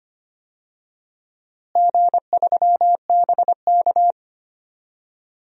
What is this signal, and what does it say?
Morse code, heard as a keyed tone, about 25 wpm, 700 Hz. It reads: G3BK
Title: DXCC-100 Call Signs - 4 Characters - Repeated 3x - 25wpm